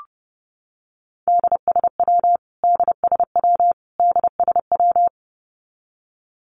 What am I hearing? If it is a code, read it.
Morse code, heard as a keyed tone, about 30 wpm, 700 Hz. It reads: DSW DSW DSW
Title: Abbreviations Repeated 3x - 30wpm